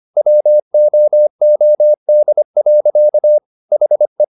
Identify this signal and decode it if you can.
Morse code, heard as a keyed tone, about 25 wpm, 600 Hz. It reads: WOOD. HE
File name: the old man and the sea part 1 25wpm